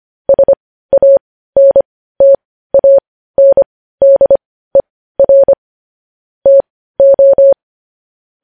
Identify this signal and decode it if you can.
Morse code, heard as a keyed tone, about 25 wpm, spaced as for 15 wpm, 550 Hz. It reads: SANTANDER TO
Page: news headlines in morse code at 15 wpm